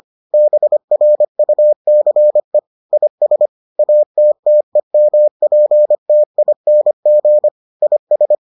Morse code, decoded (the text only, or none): BRUCE IS ATTEMPTING IS